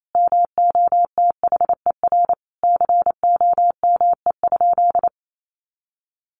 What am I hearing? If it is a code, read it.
Morse code, heard as a keyed tone, about 28 wpm, 700 Hz. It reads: MOTHER COME?
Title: Sentences from Top 200 Words 28wpm